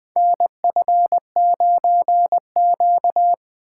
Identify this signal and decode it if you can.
Morse code, heard as a keyed tone, about 20 wpm, 700 Hz. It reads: NF9Q
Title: USA Call Signs - 4 Characters - 2x Character Spacing - 20wpm